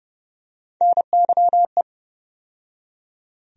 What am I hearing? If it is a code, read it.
Morse code, heard as a keyed tone, about 30 wpm, 700 Hz. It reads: NYE
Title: USA Call Sign Postfixes - 3 Letters 30wpm